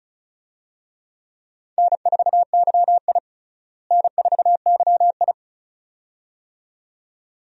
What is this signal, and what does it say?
Morse code, heard as a keyed tone, about 34 wpm, 700 Hz. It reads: N4YI N4YI
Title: USA Call Signs - 4 Characters - Repeated 2x - 35wpm